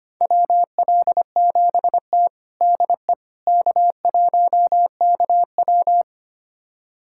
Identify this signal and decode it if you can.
Morse code, heard as a keyed tone, about 25 wpm, 700 Hz. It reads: WL7T DE K1KW
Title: USA Callsign DE Callsign - 4 Characters - 25wpm